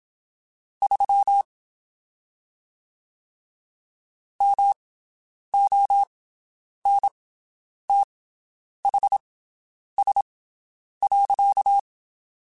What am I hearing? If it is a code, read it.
Morse code, heard as a keyed tone, about 27 wpm, spaced as for 9 wpm, 800 Hz. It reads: 3 MONTHS.
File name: (spaced) / QSOs Spaced-mp3s / WD0BGP 1 @25 sp